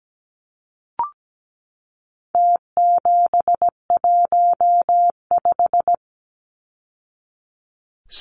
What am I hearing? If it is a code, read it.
Morse code, heard as a keyed tone, about 17 wpm, 700 Hz. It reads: T715
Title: Numbers - Contest Style - 4-Digit 17wpm